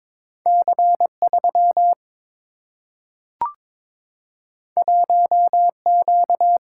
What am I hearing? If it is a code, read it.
Morse code, heard as a keyed tone, about 22 wpm, 700 Hz. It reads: C 3 1 Q
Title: Single-Letter Single-Number 22wpm